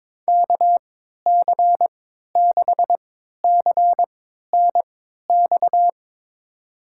Morse code, heard as a keyed tone, about 22 wpm, spaced as for 12 wpm, 700 Hz. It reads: KC6CNX
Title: USA Call Signs - 6 Characters - 3x Character Spacing - 22wpm